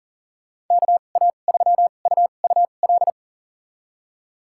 Morse code, heard as a keyed tone, about 38 wpm, spaced as for 28 wpm, 700 Hz. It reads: KA3UUL
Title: USA Call Signs - 6 Characters - 2x Character Spacing - 40wpm